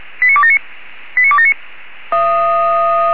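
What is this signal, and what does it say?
Five-tone selective call (ZVEI), heard as quick runs of five tones, tones of about 70 ms. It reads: 89279 89279
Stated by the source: ZVEI five-tone selective call 89270 received on an SDR.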